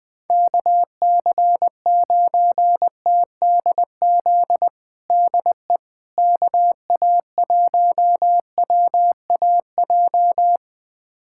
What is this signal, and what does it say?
Morse code, heard as a keyed tone, about 21 wpm, 700 Hz. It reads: KC9TDZ DE KA1WAJ
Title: USA Callsign DE Callsign - 6 Characters - 20wpm